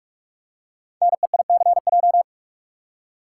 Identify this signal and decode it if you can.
Morse code, heard as a keyed tone, about 44 wpm, 700 Hz. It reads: NEICJ
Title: Code Groups - 5 Letters - 45wpm